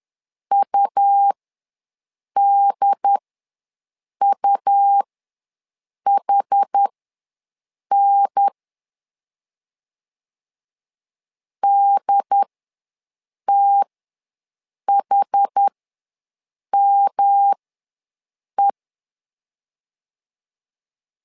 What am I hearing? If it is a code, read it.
Morse code, heard as a keyed tone, about 10 wpm, spaced as for 6 wpm, 800 Hz. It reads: UDUHN DTHME
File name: LEZIONI / 3 MWRO / 3 M ws 10wpm 2121